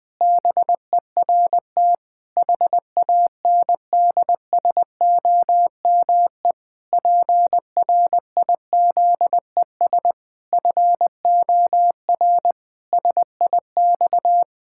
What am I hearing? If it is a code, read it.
Morse code, heard as a keyed tone, about 20 wpm, 700 Hz. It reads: BERT HANDSOME PRIZES FOR SIX
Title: Pangram Challenge v2 - 20wpm